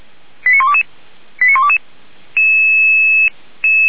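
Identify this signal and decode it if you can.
Five-tone selective call (ZVEI), heard as quick runs of five tones, tones of about 70 ms. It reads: 89122 89122
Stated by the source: Five-tone ZVEI selective call, code 89386, received by SDR.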